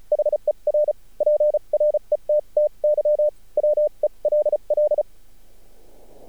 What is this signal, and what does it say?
Morse code, heard as a keyed tone, about 34 wpm, spaced as for 29 wpm, 600 Hz. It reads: HER PRETTY WELL
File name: archived / CRSnet